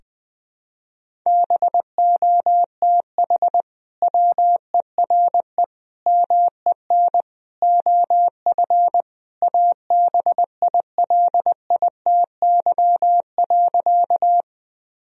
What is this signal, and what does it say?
Morse code, heard as a keyed tone, about 21 wpm, 700 Hz. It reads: BOTH WERE MEN OF ABILITY.
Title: Sentences from Top 1000 Words - Review Part 3 - 20wpm